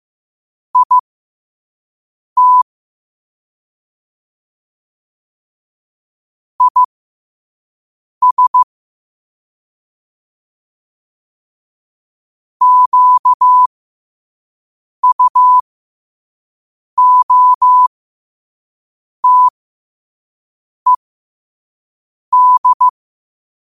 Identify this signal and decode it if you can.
Morse code, heard as a keyed tone, about 15 wpm, spaced as for 5 wpm, 1000 Hz. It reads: IT IS QUOTED
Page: Quotes for Wed, 13 Aug 2025 in Morse Code at 5 words per minute.